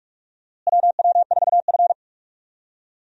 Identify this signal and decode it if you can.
Morse code, heard as a keyed tone, about 46 wpm, 700 Hz. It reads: WW4F
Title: USA Call Signs - 4 Characters - Repeated 3x - 45wpm